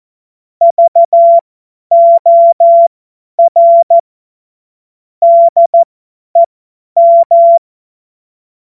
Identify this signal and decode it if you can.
Morse code, heard as a keyed tone, about 14 wpm, spaced as for 10 wpm, 680 Hz. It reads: VOR DEM
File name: Witz-01-Juli-2025-50-F70